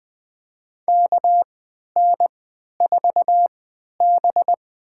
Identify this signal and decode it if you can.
Morse code, heard as a keyed tone, about 20 wpm, spaced as for 11 wpm, 700 Hz. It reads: KN4B
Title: USA Call Signs - 4 Characters - 3x Character Spacing - 20wpm